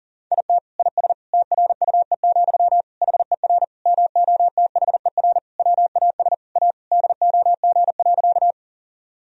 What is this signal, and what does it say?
Morse code, heard as a keyed tone, about 40 wpm, 700 Hz. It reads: IT IS TRUE, HER MOTHER WAS A DOG.
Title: Sentences from Top 1000 Words - Review Part 2 - 40wpm